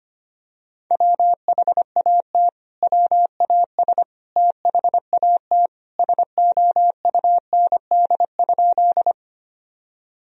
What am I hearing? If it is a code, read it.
Morse code, heard as a keyed tone, about 25 wpm, 700 Hz. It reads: WHAT WAS THAT SOUND?
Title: Sentences from Top 1000 Words - Review Part 1 - 25wpm